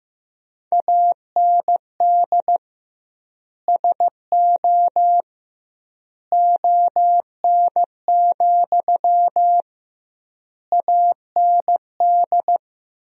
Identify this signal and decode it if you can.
Morse code, heard as a keyed tone, about 15 wpm, 700 Hz. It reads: AND SO ON, AND
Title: 2x Word-Spacing - Sentences from Top 100 Words - 15wpm